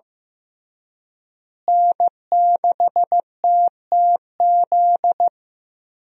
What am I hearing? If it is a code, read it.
Morse code, heard as a keyed tone, about 15 wpm, 700 Hz. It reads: N6TTZ
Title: USA Call Signs - 5 Characters - Repeated 3x - 15wpm